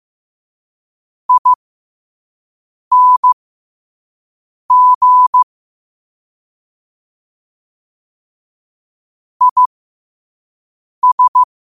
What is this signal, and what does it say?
Morse code, heard as a keyed tone, about 15 wpm, spaced as for 5 wpm, 1000 Hz. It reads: ING IS
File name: qotdfiles / 5wpm